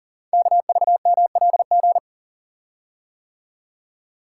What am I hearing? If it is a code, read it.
Morse code, heard as a keyed tone, about 43 wpm, 700 Hz. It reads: KVMLG
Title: Code Groups - 5 Letters - 40wpm